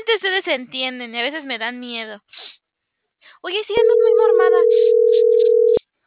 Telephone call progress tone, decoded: ringback tone

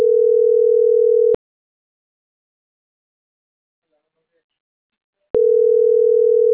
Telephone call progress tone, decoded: ringback tone

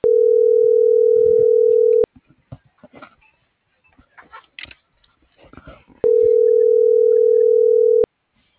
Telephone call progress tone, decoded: ringback tone

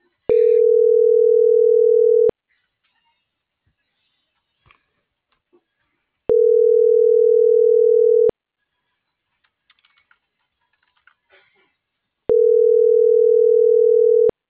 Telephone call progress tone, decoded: ringback tone

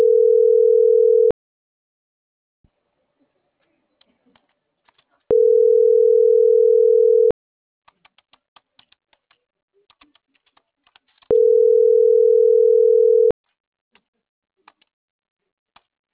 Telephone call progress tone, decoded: ringback tone

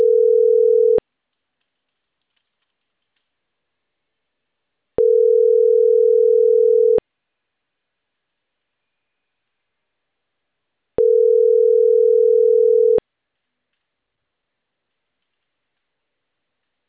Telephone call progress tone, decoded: ringback tone

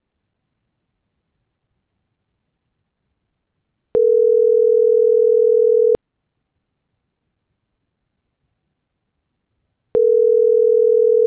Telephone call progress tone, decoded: ringback tone

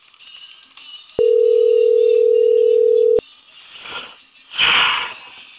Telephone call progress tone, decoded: ringback tone

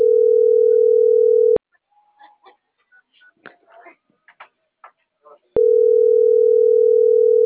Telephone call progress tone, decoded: ringback tone

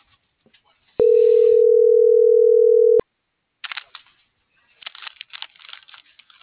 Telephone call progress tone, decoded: ringback tone

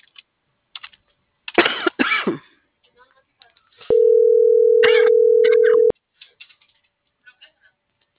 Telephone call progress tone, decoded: ringback tone

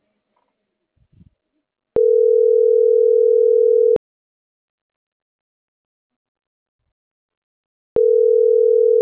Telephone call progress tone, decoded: ringback tone

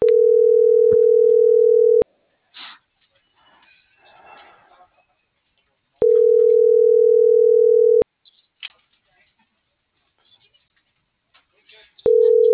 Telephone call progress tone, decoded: ringback tone

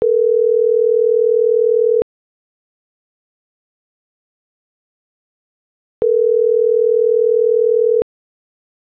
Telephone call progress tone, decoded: ringback tone

